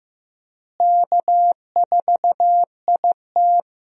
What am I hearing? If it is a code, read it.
Morse code, heard as a keyed tone, about 16 wpm, 700 Hz. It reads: K4IT
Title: USA Call Signs - 4 Characters - Repeated 3x - 15wpm